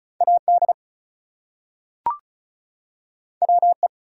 Morse code, heard as a keyed tone, about 36 wpm, 700 Hz. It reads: AD WE
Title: Two Letter Words - 35wpm